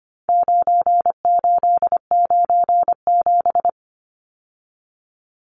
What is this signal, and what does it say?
Morse code, heard as a keyed tone, about 25 wpm, 700 Hz. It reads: 9897
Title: Numbers - 4-Digit 25wpm